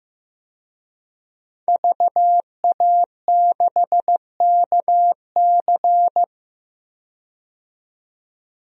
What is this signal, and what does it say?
Morse code, heard as a keyed tone, about 15 wpm, 700 Hz. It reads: VA6KC
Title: DXCC-100 Call Signs - 5 Characters - 15wpm